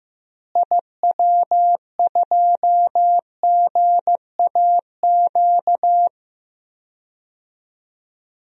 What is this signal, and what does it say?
Morse code, heard as a keyed tone, about 15 wpm, 700 Hz. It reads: IW2GAQ
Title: DXCC-100 Call Signs - 6 Characters - Repeated 3x - 15wpm